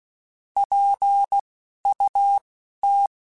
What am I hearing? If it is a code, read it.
Morse code, heard as a keyed tone, about 16 wpm, spaced as for 11 wpm, 800 Hz. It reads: PUT
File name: Spaced / Chuckles Spaced-mp3s / BrainMouth sp @15